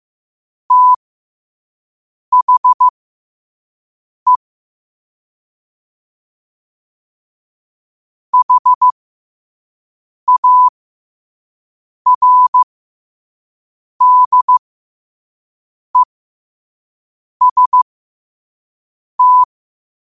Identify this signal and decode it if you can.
Morse code, heard as a keyed tone, about 15 wpm, spaced as for 5 wpm, 1000 Hz. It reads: THE HARDEST